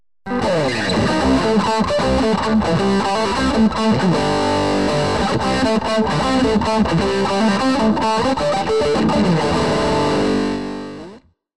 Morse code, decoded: none